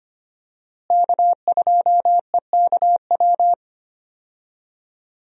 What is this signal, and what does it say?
Morse code, heard as a keyed tone, about 25 wpm, 700 Hz. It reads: K2EKW
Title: USA Call Signs - 5 Characters - Repeated 3x - 25wpm